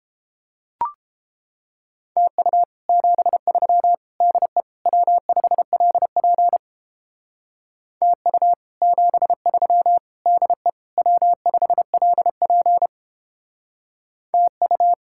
Morse code, heard as a keyed tone, about 29 wpm, 700 Hz. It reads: TU 73 DE W5LP TU 73 DE W5LP TU